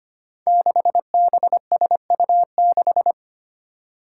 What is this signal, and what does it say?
Morse code, heard as a keyed tone, about 25 wpm, 700 Hz. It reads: 6BSU6
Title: Five Challenging Letter-Number Combinations 25wpm